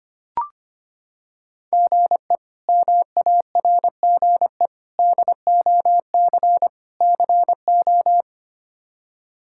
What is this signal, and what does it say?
Morse code, heard as a keyed tone, about 25 wpm, 700 Hz. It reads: GE MARGE DOC CO
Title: K1USN SST QSO Practice - 25wpm